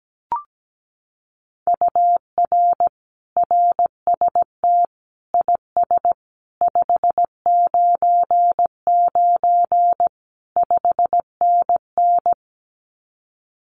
Morse code, heard as a keyed tone, about 17 wpm, 700 Hz. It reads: UR RST IS 599 5NN
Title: Common QSO Phrases Encore 17wpm